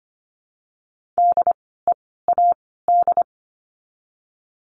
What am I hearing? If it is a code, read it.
Morse code, heard as a keyed tone, about 25 wpm, 700 Hz. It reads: D E A D
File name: course / 25:10wpm / 094 Lesson 16f - Letter c - Words 25wpm